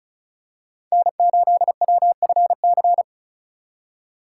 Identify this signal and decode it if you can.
Morse code, heard as a keyed tone, about 34 wpm, 700 Hz. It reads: N8WFC
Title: USA Call Signs - 5 Characters - 3x Character Spacing - 35wpm